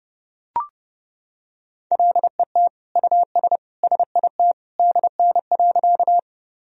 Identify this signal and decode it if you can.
Morse code, heard as a keyed tone, about 30 wpm, 700 Hz. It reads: LET US SIT DN.
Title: Sentences from Top 500 Words and Abbreviations 30wpm